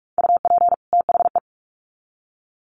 Morse code, heard as a keyed tone, about 44 wpm, 700 Hz. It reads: UP THE
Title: Sets of 2 Words 45wpm